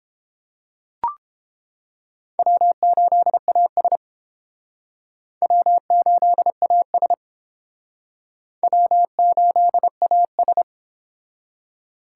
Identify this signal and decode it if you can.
Morse code, heard as a keyed tone, about 30 wpm, 700 Hz. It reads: W8AS W8AS W8AS